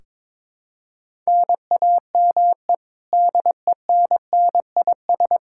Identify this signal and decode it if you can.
Morse code, heard as a keyed tone, about 21 wpm, 700 Hz. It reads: NAME DENNIS
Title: Practice QSO - Exchange 1 - Beginning - 22wpm